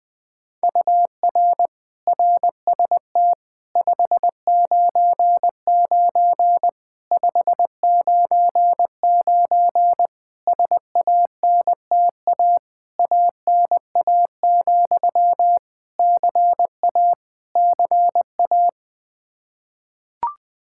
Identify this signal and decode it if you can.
Morse code, heard as a keyed tone, about 20 wpm, 700 Hz. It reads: UR RST 599 599 SANTA ANA, CA CA
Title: Practice QSO - Exchange 1 - Beginning - 20wpm